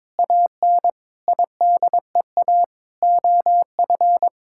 Morse code, heard as a keyed tone, about 22 wpm, 700 Hz. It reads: AN IDEA OF
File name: practice / 22wpm / Sets of 7 Words - 22wpm